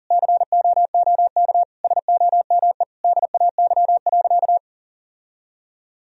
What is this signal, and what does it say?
Morse code, heard as a keyed tone, about 40 wpm, 700 Hz. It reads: COOK SOME DAY.
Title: Sentences from Top 1000 Words - Review Part 4 - 40wpm